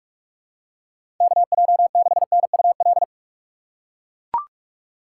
Morse code, heard as a keyed tone, about 42 wpm, 700 Hz. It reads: KJ6NUL
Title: USA Call Signs - 6 Characters - 4x Character Spacing - 45wpm